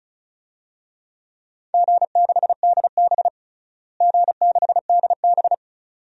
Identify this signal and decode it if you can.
Morse code, heard as a keyed tone, about 36 wpm, 700 Hz. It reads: G6DB G6DB
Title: DXCC-100 Call Signs - 4 Characters - Repeated 2x - 35wpm